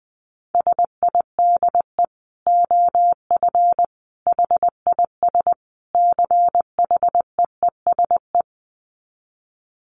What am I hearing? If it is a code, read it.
Morse code, heard as a keyed tone, about 20 wpm, 700 Hz. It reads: SIDE OF HIS CHEESE
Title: Dit Challenge 20wpm